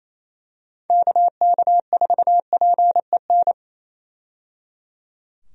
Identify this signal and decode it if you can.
Morse code, heard as a keyed tone, about 28 wpm, 700 Hz. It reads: KK4PEN
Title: USA Call Signs - 6 Characters - 28wpm